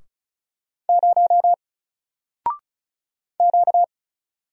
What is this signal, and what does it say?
Morse code, heard as a keyed tone, about 37 wpm, 700 Hz. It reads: 0 Q
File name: Single-Letter-Number-35wpm